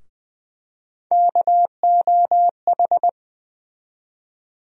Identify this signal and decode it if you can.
Morse code, heard as a keyed tone, about 20 wpm, 700 Hz. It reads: KOH
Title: USA Call Sign Postfixes - 3 Letters 20wpm